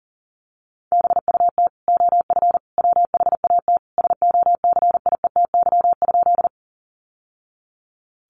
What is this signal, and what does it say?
Morse code, heard as a keyed tone, about 40 wpm, 700 Hz. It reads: BUT OF WHAT SOCIETY?
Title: Sentences from Top 700 Words 40wpm